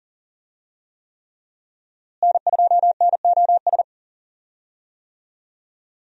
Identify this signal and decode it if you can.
Morse code, heard as a keyed tone, about 37 wpm, 700 Hz. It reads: N2NOS
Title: USA Call Signs - 5 Characters - Repeated 3x - 40wpm